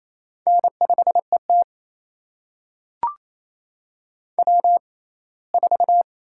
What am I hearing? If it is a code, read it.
Morse code, heard as a keyed tone, about 27 wpm, spaced as for 19 wpm, 700 Hz. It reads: N5ET W4
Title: USA Call Signs - 4 Characters - 6x Character Spacing - 28wpm